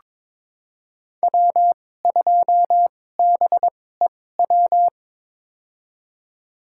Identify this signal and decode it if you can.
Morse code, heard as a keyed tone, about 22 wpm, spaced as for 16 wpm, 700 Hz. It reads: W2BEW
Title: USA Call Signs - 5 Characters - 2x Character Spacing - 22wpm